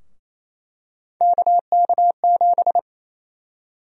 Morse code, heard as a keyed tone, about 28 wpm, 700 Hz. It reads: KK7
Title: USA Call Sign Prefixes - Repeated 2x - 28wpm